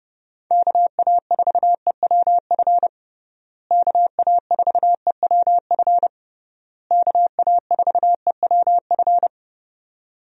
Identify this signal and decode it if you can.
Morse code, heard as a keyed tone, about 32 wpm, 700 Hz. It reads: KA4EWF KA4EWF KA4EWF
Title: USA Call Signs - 6 Characters - Repeated 3x - 30wpm